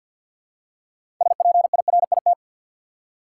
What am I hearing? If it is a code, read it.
Morse code, heard as a keyed tone, about 53 wpm, 700 Hz. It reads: SPIRIT
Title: Six Letter Words - 50wpm